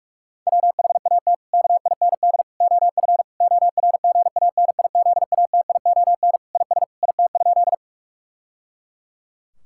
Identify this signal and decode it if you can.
Morse code, heard as a keyed tone, about 42 wpm, 700 Hz. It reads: WHAT KIND OF ORGANIZATION IS IT?
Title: Sentences from Top 1000 Words - Review Part 5 - 45wpm